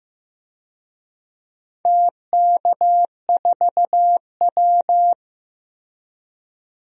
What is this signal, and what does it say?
Morse code, heard as a keyed tone, about 15 wpm, 700 Hz. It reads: TK4W
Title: DXCC-100 Call Signs - 4 Characters - Repeated 3x - 15wpm